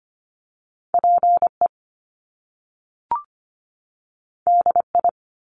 Morse code, heard as a keyed tone, about 25 wpm, 700 Hz. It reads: PE DI
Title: Top Two Letter Combinations in Words 25wpm